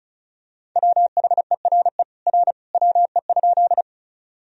Morse code, heard as a keyed tone, about 36 wpm, 700 Hz. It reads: WHERE R WE?
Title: Sentences from Top 200 Words and Abbreviations 35wpm